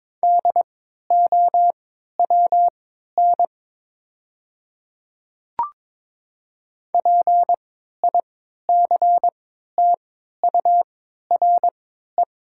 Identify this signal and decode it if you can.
Morse code, heard as a keyed tone, about 22 wpm, spaced as for 13 wpm, 700 Hz. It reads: DOWN PICTURE
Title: 3x Character Spacing - Top 200 Words - 22wpm